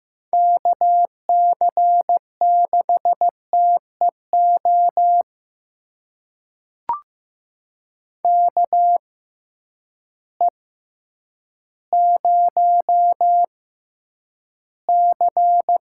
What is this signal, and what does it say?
Morse code, heard as a keyed tone, about 15 wpm, spaced as for 9 wpm, 700 Hz. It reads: KC6TEO KE0C
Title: USA Call Signs - 6 Characters - 6x Character Spacing - 15wpm